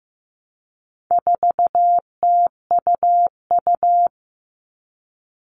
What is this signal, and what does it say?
Morse code, heard as a keyed tone, about 15 wpm, 700 Hz. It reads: 4TUU
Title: All Cut Numbers - 4-Digit 15wpm